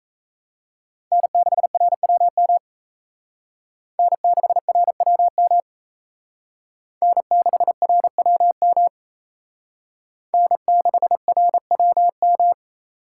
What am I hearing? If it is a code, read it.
Morse code, heard as a keyed tone, about 35 wpm, 700 Hz. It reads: N6RWM N6RWM N6RWM N6RWM
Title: Speed Racing - USA 1x3 Call Signs 28wpm